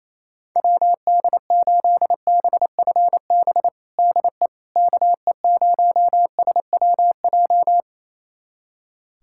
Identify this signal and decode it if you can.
Morse code, heard as a keyed tone, about 27 wpm, 700 Hz. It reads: WD8BFB DE KE0SWJ
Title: USA Callsign DE Callsign - 6 Characters - 28wpm